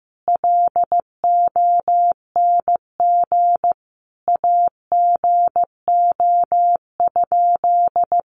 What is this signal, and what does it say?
Morse code, heard as a keyed tone, about 14 wpm, 700 Hz. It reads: LONG AGO?
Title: Sentences from Top 500 Words 15wpm